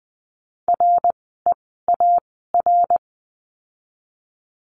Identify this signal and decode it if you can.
Morse code, heard as a keyed tone, about 20 wpm, spaced as for 14 wpm, 700 Hz. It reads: REAR
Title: 054 Lesson 10f - Letter r - Words 20wpm